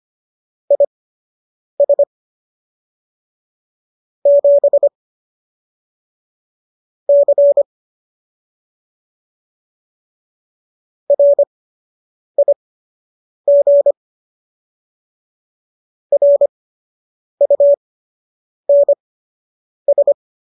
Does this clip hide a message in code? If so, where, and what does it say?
Yes: Morse code from 0.7 to 20.1 s, heard as a keyed tone, about 25 wpm, spaced as for 8 wpm, 570 Hz. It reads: IS 7 C RIG RUNS